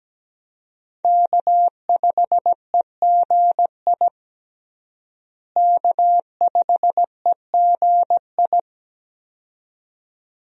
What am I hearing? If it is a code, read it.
Morse code, heard as a keyed tone, about 17 wpm, 700 Hz. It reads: K5EGI K5EGI
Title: USA Call Signs - 5 Characters - Repeated 2x - 17wpm